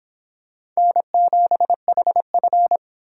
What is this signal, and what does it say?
Morse code, heard as a keyed tone, about 26 wpm, 700 Hz. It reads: N7HF